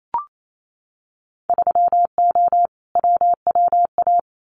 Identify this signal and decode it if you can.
Morse code, heard as a keyed tone, about 28 wpm, 700 Hz. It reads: 3O WWA
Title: Winter Field Day Exchanges Encore 28wpm